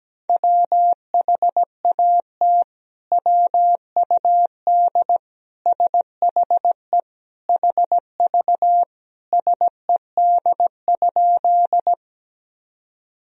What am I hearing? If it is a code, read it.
Morse code, heard as a keyed tone, about 17 wpm, 700 Hz. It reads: WHAT WUD SHE HV SED?
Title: Sentences from Top 100 Words and Abbreviations 17wpm